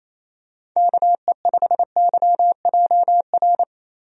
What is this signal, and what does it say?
Morse code, heard as a keyed tone, about 28 wpm, 700 Hz. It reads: KE5YJR
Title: USA Call Signs - 6 Characters - Repeated 3x - 28wpm